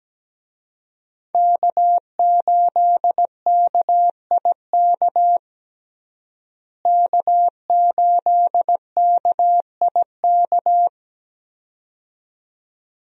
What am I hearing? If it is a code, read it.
Morse code, heard as a keyed tone, about 17 wpm, 700 Hz. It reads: K8KIK K8KIK
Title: USA Call Signs - 5 Characters - Repeated 2x - 17wpm